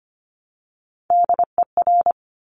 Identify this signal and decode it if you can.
Morse code, heard as a keyed tone, about 25 wpm, 700 Hz. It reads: DER
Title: Top Three Letter Combinations in Words 25wpm